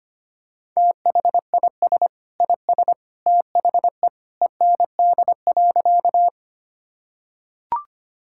Morse code, heard as a keyed tone, about 25 wpm, 700 Hz. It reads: THIS IS THE END.
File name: practice / 25wpm / Sentences with 2-Word Phrases - 25wpm